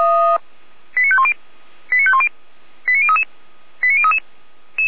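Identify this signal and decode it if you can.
Five-tone selective call (ZVEI), heard as quick runs of five tones, tones of about 70 ms. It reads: 89510 89510 89033 89033
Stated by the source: ZVEI five-tone selective call received via SDR, from the Feuerwehr Orlamünde audio server.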